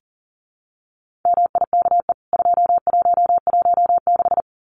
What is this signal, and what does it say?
Morse code, heard as a keyed tone, about 40 wpm, 700 Hz. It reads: MIKE 2116
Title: CWT Exchange 40wpm